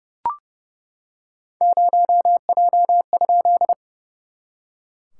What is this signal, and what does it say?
Morse code, heard as a keyed tone, about 30 wpm, 690 Hz. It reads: 0J?
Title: SOTA QSO Practice 30wpm